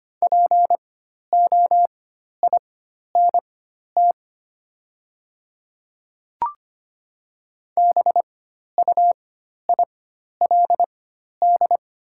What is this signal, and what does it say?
Morse code, heard as a keyed tone, about 25 wpm, spaced as for 12 wpm, 690 Hz. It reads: POINT BUILD
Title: 4x Character Spacing - Top 200 Words - 25wpm